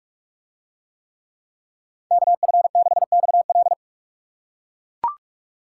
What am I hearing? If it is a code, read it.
Morse code, heard as a keyed tone, about 40 wpm, 700 Hz. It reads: KF6XL